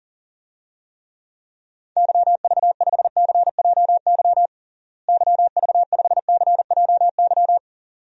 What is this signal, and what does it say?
Morse code, heard as a keyed tone, about 40 wpm, 700 Hz. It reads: YV5CJY YV5CJY
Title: DXCC-100 Call Signs - 6 Characters - Repeated 2x - 40wpm